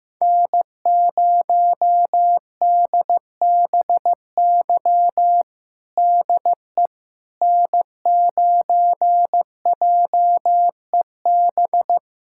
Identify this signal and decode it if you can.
Morse code, heard as a keyed tone, about 15 wpm, 700 Hz. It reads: N0DBY DE N9JEB
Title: USA Callsign DE Callsign 15wpm